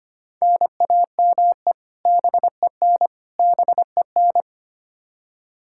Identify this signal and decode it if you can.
Morse code, heard as a keyed tone, about 25 wpm, 700 Hz. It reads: NAME BEN BEN
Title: Practice QSO - Exchange 1 - Beginning - 25wpm